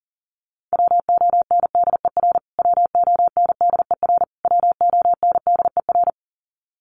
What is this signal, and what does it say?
Morse code, heard as a keyed tone, about 37 wpm, 700 Hz. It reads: WONDER WONDER WONDER
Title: Top 500 Words Repeated 40wpm